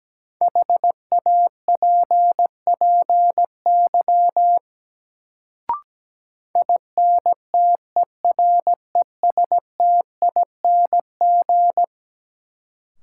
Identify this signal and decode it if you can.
Morse code, heard as a keyed tone, about 17 wpm, 700 Hz. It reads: HAPPY INTERESTING
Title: Sentences from Top 1000 Words - New Word Drill - 17wpm